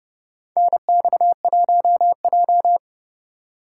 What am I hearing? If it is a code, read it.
Morse code, heard as a keyed tone, about 30 wpm, 700 Hz. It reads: NX1J
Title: USA Call Signs - 4 Characters - 3x Character Spacing - 30wpm